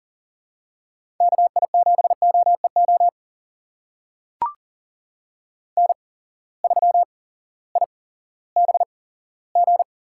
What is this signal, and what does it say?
Morse code, heard as a keyed tone, about 40 wpm, spaced as for 18 wpm, 700 Hz. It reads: KI7OEO N3IBG
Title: USA Call Signs - 8x Character Spacing - 40wpm